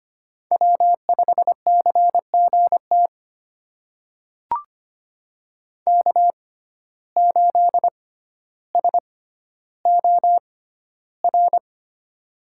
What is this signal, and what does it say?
Morse code, heard as a keyed tone, about 25 wpm, spaced as for 13 wpm, 700 Hz. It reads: W5CGT K8SOR
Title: USA Call Signs - 5 Characters - 6x Character Spacing - 25wpm